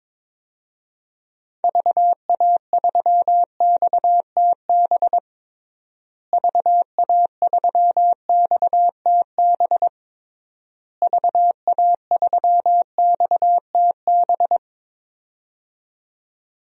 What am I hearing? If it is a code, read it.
Morse code, heard as a keyed tone, about 22 wpm, 700 Hz. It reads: VA3XTB VA3XTB VA3XTB
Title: DXCC-100 Call Signs - 6 Characters - Repeated 3x - 22wpm